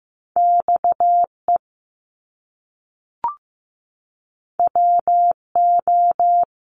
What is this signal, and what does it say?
Morse code, heard as a keyed tone, about 15 wpm, 700 Hz. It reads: XE WO
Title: Top Two Letter Combinations in Words 15wpm